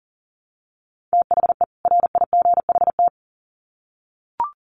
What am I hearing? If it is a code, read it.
Morse code, heard as a keyed tone, about 40 wpm, 700 Hz. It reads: THE RIGHT
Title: Sets of 2 Words 40wpm